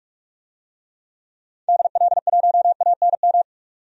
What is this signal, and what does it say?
Morse code, heard as a keyed tone, about 43 wpm, 700 Hz. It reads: DL1ANM
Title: DXCC-100 Call Signs - 6 Characters - 45wpm